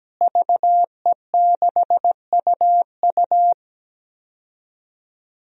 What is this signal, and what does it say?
Morse code, heard as a keyed tone, about 17 wpm, 700 Hz. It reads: VE6UU
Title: DXCC-100 Call Signs - 5 Characters - Repeated 2x - 17wpm